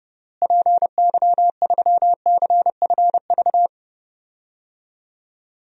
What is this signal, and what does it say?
Morse code, heard as a keyed tone, about 32 wpm, 700 Hz. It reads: PY3CFV
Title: DXCC-100 Call Signs - 6 Characters - Repeated 2x - 30wpm